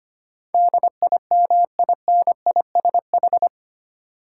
Morse code, heard as a keyed tone, about 25 wpm, 700 Hz. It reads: DIMINISH
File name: Eight Letter Words - 25wpm